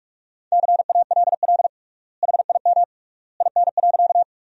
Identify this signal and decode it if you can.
Morse code, heard as a keyed tone, about 44 wpm, 700 Hz. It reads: CALL HIM IN.
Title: 3x Word-Spacing - Sentences from Top 100 Words - 45wpm